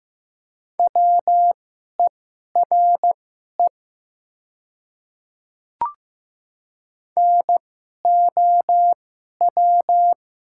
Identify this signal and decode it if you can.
Morse code, heard as a keyed tone, about 15 wpm, spaced as for 11 wpm, 700 Hz. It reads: WERE NOW
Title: 2x Character Spacing - Top 100 Words - 15wpm